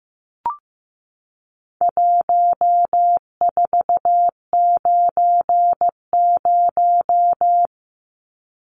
Morse code, heard as a keyed tone, about 15 wpm, 700 Hz. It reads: 1490
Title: Numbers - 4-Digit 15wpm